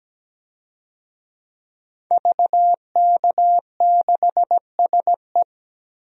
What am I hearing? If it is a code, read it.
Morse code, heard as a keyed tone, about 17 wpm, 700 Hz. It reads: VK6SE
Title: DXCC-100 Call Signs - 5 Characters - 17wpm